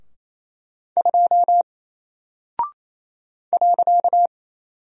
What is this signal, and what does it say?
Morse code, heard as a keyed tone, about 28 wpm, 700 Hz. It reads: 2 .
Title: Single Character (Letter:Number:Punctuation:Prosign) Alt 28wpm